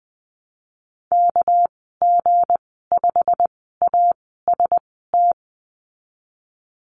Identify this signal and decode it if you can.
Morse code, heard as a keyed tone, about 20 wpm, spaced as for 15 wpm, 700 Hz. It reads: KG5AST
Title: 238 Lesson 38g - Letter q - Call Signs 20wpm